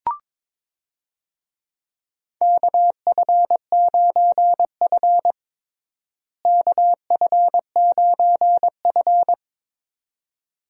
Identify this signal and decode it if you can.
Morse code, heard as a keyed tone, about 21 wpm, 700 Hz. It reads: KF9F KF9F
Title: USA Call Signs - 4 Characters - Repeated 2x - 22wpm